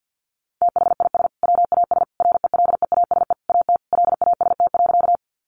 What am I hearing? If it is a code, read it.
Morse code, heard as a keyed tone, about 50 wpm, 700 Hz. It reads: THIS WAS RELEASE AT LAST.
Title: Sentences from Top 1100 Words 50wpm